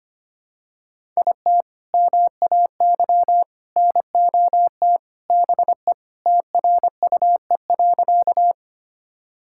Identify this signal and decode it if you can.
Morse code, heard as a keyed tone, about 25 wpm, 700 Hz. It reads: IT MAY NOT BE TRUE.
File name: practice / 25wpm / Sentences with 3-Word Phrases - 25wpm